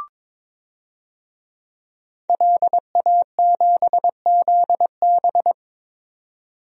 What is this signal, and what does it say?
Morse code, heard as a keyed tone, about 22 wpm, 700 Hz. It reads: LA7ZB
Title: DXCC-100 Call Signs - 5 Characters - 22wpm